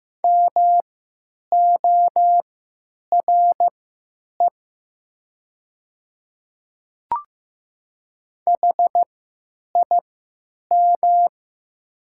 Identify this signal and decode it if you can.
Morse code, heard as a keyed tone, about 15 wpm, spaced as for 9 wpm, 690 Hz. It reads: MORE HIM
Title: 3x Character Spacing - Top 100 Words - 15wpm